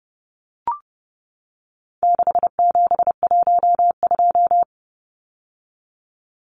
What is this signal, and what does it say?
Morse code, heard as a keyed tone, about 30 wpm, 700 Hz. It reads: 6712
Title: Numbers - 4-Digit 30wpm